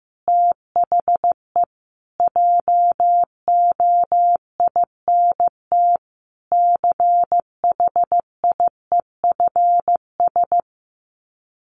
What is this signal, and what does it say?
Morse code, heard as a keyed tone, about 15 wpm, 700 Hz. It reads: THE JOINT CHIEFS
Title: Sets of 5 Words Encore 15wpm